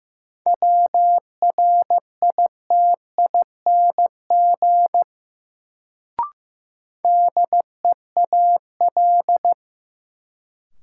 Morse code, heard as a keyed tone, about 15 wpm, 700 Hz. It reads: WRITING DEAL
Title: Sentences from Top 800 Words - New Word Drill - 15wpm